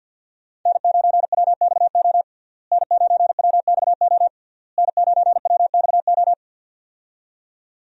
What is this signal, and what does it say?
Morse code, heard as a keyed tone, about 50 wpm, 700 Hz. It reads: N9WXO N9WXO N9WXO
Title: USA Call Signs - 5 Characters - Repeated 3x - 50wpm